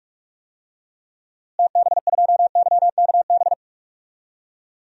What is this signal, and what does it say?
Morse code, heard as a keyed tone, about 47 wpm, 700 Hz. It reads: TB2YKB
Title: DXCC-100 Call Signs - 6 Characters - Repeated 2x - 45wpm